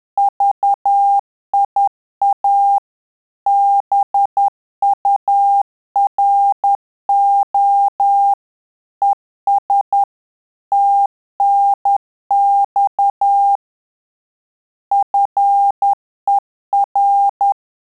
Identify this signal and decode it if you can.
Morse code, heard as a keyed tone, about 10 wpm, 800 Hz. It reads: VIA BURO ES TNX FER
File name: Telegraphy-group / QSO CW 10 PPM